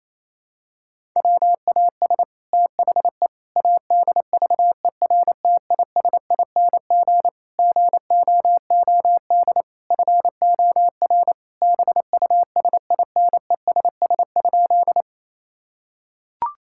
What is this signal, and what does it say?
Morse code, heard as a keyed tone, about 28 wpm, 700 Hz. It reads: WAS THE ADVERTISING GOOD FOR BUSINESS?
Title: Sentences from Top 1000 Words - Review Part 6 - 28wpm